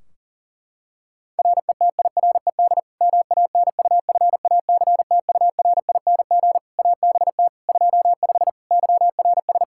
Morse code, heard as a keyed tone, about 40 wpm, 700 Hz. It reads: RETIRED MANUFACTURING ABT 25 YRS
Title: K1USN SST QSO Practice 40wpm